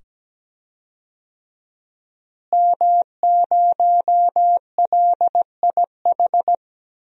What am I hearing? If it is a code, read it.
Morse code, heard as a keyed tone, about 17 wpm, 700 Hz. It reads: M0LIH
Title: DXCC-100 Call Signs - 5 Characters - Repeated 2x - 17wpm